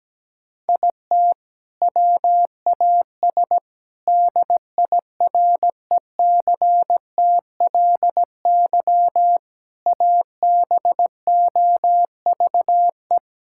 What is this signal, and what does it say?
Morse code, heard as a keyed tone, about 17 wpm, 700 Hz. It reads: IT WAS DIRECTLY ABOVE
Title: Sentences from Top 1000 Words - Review Part 5 - 17wpm